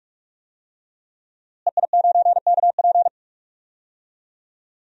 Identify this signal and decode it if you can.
Morse code, heard as a keyed tone, about 46 wpm, 700 Hz. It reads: EI9KP
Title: DXCC-100 Call Signs - 5 Characters - Repeated 2x - 45wpm